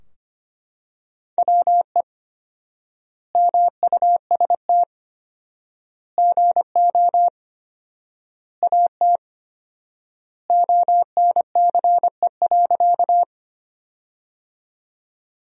Morse code, heard as a keyed tone, about 25 wpm, 700 Hz. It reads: WE MUST GO AT ONCE.
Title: 4x Word-Spacing - Sentences from Top 300 Words - 25wpm